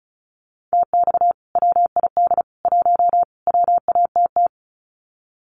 Morse code, heard as a keyed tone, about 34 wpm, 700 Hz. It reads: TX WID 1 WATT
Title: Common QSO Phrases 35wpm